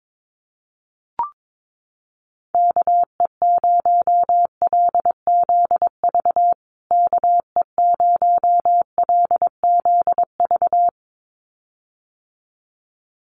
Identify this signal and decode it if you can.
Morse code, heard as a keyed tone, about 22 wpm, 700 Hz. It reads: KE0LZV KE0LZV
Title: USA Call Signs Repeated 2x 22wpm